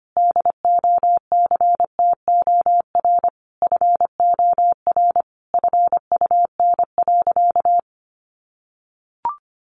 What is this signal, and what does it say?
Morse code, heard as a keyed tone, about 25 wpm, 700 Hz. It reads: DOCTOR FOR FUN.
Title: Additional Easy Phrases 25wpm